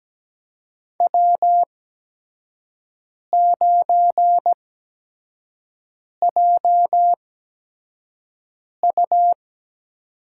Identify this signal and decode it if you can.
Morse code, heard as a keyed tone, about 17 wpm, spaced as for 5 wpm, 700 Hz. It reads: W9JU
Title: USA Call Signs - 4 Characters - 8x Character Spacing - 17wpm